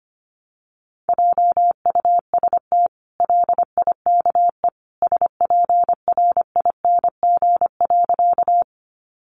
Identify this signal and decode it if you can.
Morse code, heard as a keyed tone, about 25 wpm, 700 Hz. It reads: JUST LIKE SPRING.
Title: Sentences from Top 1200 Words 25wpm